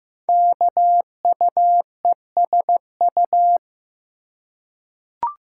Morse code, heard as a keyed tone, about 15 wpm, 700 Hz. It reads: KUESU
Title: Code Groups - 5 Letter:Numbers - 15wpm